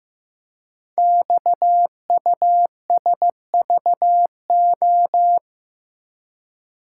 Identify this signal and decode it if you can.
Morse code, heard as a keyed tone, about 15 wpm, 700 Hz. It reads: XUSVO
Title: Code Groups - 5 Letter:Numbers - 15wpm